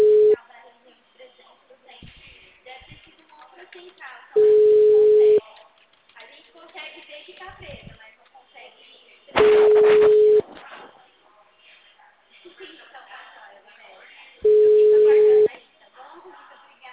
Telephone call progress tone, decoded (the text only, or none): ringback tone